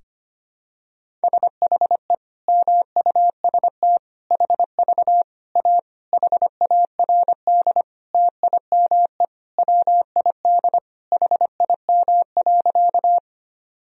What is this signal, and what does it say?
Morse code, heard as a keyed tone, about 25 wpm, 700 Hz. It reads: SHE MUST HV A HARD TIME WID HIM.
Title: Sentences from Top 300 Words and Abbreviations 25wpm